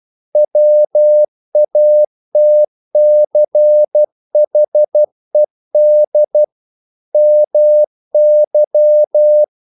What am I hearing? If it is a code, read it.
Morse code, heard as a keyed tone, about 12 wpm, 600 Hz. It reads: WATCHED MY
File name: mp3 files / the gardener - 12wpm